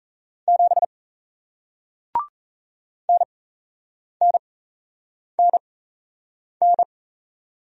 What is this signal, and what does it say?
Morse code, heard as a keyed tone, about 38 wpm, spaced as for 8 wpm, 700 Hz. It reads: 7 NNNN